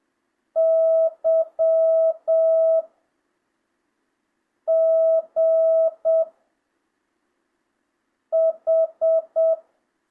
Morse code, heard as a keyed tone, about 7 wpm, 660 Hz. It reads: Y G H
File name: jh8cbh / 2025cw